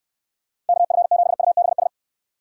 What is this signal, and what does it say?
Morse code, heard as a keyed tone, about 64 wpm, 700 Hz. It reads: KF7UBS